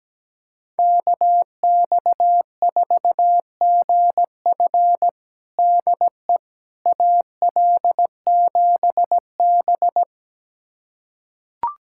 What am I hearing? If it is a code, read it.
Morse code, heard as a keyed tone, about 17 wpm, 700 Hz. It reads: KX4GF DE AL7B
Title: USA Callsign DE Callsign 17wpm